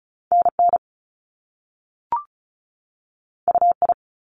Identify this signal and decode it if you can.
Morse code, heard as a keyed tone, about 36 wpm, 700 Hz. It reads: NN UI
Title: Top Two Letter Combinations in Words 35wpm